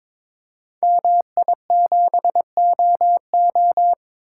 Morse code, heard as a keyed tone, about 23 wpm, 700 Hz. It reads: MI7OO
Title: Code Groups - 5 Letter:Numbers - 22wpm